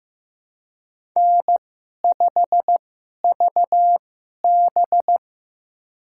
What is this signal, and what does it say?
Morse code, heard as a keyed tone, about 15 wpm, spaced as for 11 wpm, 700 Hz. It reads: N5VB
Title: USA Call Signs - 4 Characters - 2x Character Spacing - 15wpm